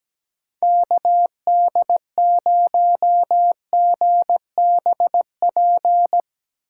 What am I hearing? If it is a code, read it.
Morse code, heard as a keyed tone, about 17 wpm, 700 Hz. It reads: KD0GBP
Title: USA Call Signs - 8x Character Spacing - 17wpm